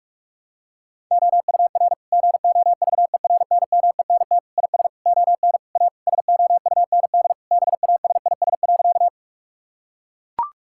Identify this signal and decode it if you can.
Morse code, heard as a keyed tone, about 41 wpm, 700 Hz. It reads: OUR GOVERNMENT IS ON A SOUND BASIS.
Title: Sentences from Top 1000 Words - Review Part 5 - 45wpm